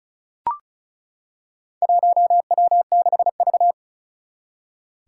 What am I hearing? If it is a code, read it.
Morse code, heard as a keyed tone, about 36 wpm, 700 Hz. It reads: 1W6V
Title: Four Challenging Letter-Number Combinations 35wpm